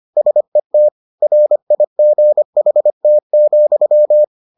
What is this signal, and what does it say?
Morse code, heard as a keyed tone, about 25 wpm, 600 Hz. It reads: SET RIGHT,